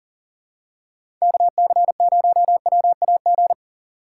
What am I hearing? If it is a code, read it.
Morse code, heard as a keyed tone, about 40 wpm, 700 Hz. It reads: KC0WAG
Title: USA Call Signs - 6 Characters - 3x Character Spacing - 40wpm